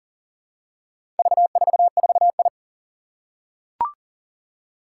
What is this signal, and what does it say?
Morse code, heard as a keyed tone, about 40 wpm, 700 Hz. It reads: V44I
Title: Four Challenging Letter-Number Combinations 40wpm